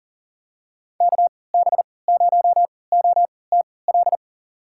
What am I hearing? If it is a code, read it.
Morse code, heard as a keyed tone, about 40 wpm, spaced as for 23 wpm, 700 Hz. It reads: KB0OTL
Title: USA Call Signs - 6 Characters - 3x Character Spacing - 40wpm